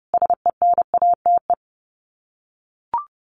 Morse code, heard as a keyed tone, about 30 wpm, 700 Hz. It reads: SENATE
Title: Sets of 5 Words Encore2 30wpm